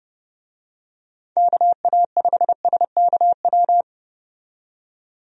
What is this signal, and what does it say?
Morse code, heard as a keyed tone, about 30 wpm, 700 Hz. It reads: KA5SKW
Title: USA Call Signs - 6 Characters - 30wpm